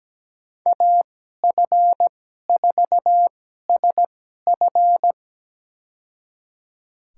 Morse code, heard as a keyed tone, about 17 wpm, spaced as for 12 wpm, 700 Hz. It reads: AF4SF
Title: USA Call Signs - 5 Characters - 2x Character Spacing - 17wpm